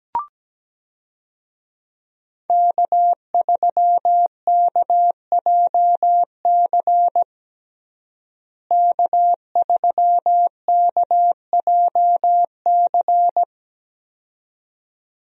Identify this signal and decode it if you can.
Morse code, heard as a keyed tone, about 17 wpm, 700 Hz. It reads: K3KJC K3KJC
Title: USA Call Signs - 5 Characters - Repeated 2x - 17wpm